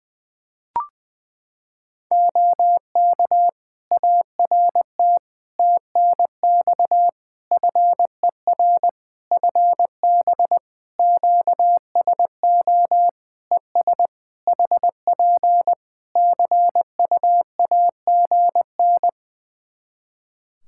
Morse code, heard as a keyed tone, about 21 wpm, 700 Hz. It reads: OK ART TNX FER FB QSO ES HP CUAGN
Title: Practice QSO - Exchange 4 - Ending - 20wpm